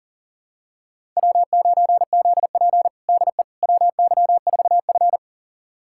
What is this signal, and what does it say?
Morse code, heard as a keyed tone, about 40 wpm, 700 Hz. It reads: W9ZP DE WY4F
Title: USA Callsign DE Callsign - 4 Characters - 40wpm